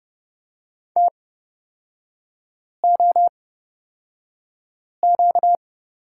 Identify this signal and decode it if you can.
Morse code, heard as a keyed tone, about 30 wpm, spaced as for 5 wpm, 700 Hz. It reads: TOQ
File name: course / 30:10wpm / 236 Lesson 38e - Letter q - Three Characters 30wpm